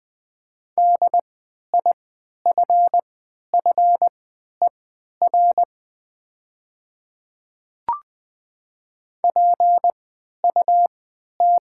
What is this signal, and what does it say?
Morse code, heard as a keyed tone, about 20 wpm, spaced as for 11 wpm, 700 Hz. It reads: DIFFER PUT
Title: 3x Character Spacing - Top 200 Words - 20wpm